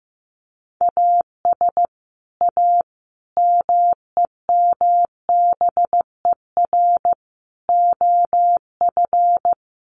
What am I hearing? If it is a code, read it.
Morse code, heard as a keyed tone, about 15 wpm, 700 Hz. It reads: AS A MEMBER OF
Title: Sets of 5 Words 15wpm